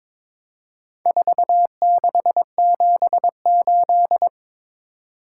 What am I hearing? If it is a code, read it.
Morse code, heard as a keyed tone, about 22 wpm, 700 Hz. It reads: 4678
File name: practice / 22wpm / Four Letter-Number Combinations - 22wpm